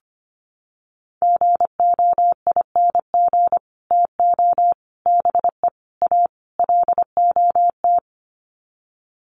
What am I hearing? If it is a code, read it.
Morse code, heard as a keyed tone, about 25 wpm, 700 Hz. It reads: GOING TO BE A LOT
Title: Sets of 5 Words Encore 25wpm